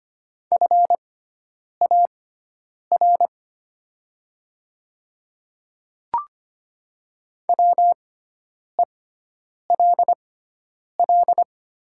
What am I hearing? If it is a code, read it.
Morse code, heard as a keyed tone, about 25 wpm, spaced as for 9 wpm, 700 Hz. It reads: FAR WELL
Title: 6x Character Spacing - Top 300 Words - 25wpm